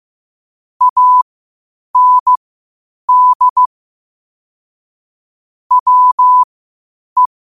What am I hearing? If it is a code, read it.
Morse code, heard as a keyed tone, about 15 wpm, spaced as for 8 wpm, 1000 Hz. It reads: AND WE
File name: qotdfiles / 8wpm